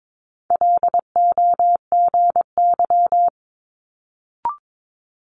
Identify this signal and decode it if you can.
Morse code, heard as a keyed tone, about 22 wpm, 700 Hz. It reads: LOGY
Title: Multi-letter Phonemes Quick Fire 22wpm